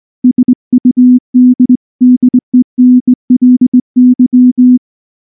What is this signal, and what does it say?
Morse code, heard as a keyed tone, about 18 wpm, 250 Hz. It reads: SUDDENLY
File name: morse / morse-training-20WPM-2